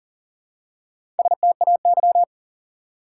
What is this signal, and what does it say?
Morse code, heard as a keyed tone, about 40 wpm, 700 Hz. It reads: STAY
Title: Four Letter Words - 40wpm